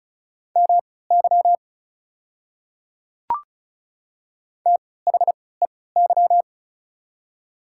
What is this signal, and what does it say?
Morse code, heard as a keyed tone, about 34 wpm, spaced as for 20 wpm, 690 Hz. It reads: MY THEY
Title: 3x Character Spacing - Top 100 Words - 35wpm